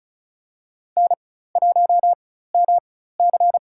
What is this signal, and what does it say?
Morse code, heard as a keyed tone, about 34 wpm, spaced as for 16 wpm, 700 Hz. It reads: N1MC
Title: USA Call Signs - 4 Characters - 4x Character Spacing - 35wpm